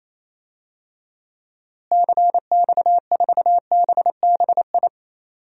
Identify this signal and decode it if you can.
Morse code, heard as a keyed tone, about 28 wpm, 700 Hz. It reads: CX4BBI
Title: DXCC-100 Call Signs - 6 Characters - Repeated 2x - 28wpm